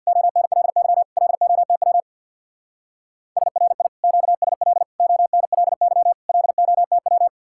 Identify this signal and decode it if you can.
Morse code, heard as a keyed tone, about 61 wpm, 700 Hz. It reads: ONLY LOTW SRI QSL ONLY LOTW
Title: Speed-Racing Common QSO Phrases - 50wpm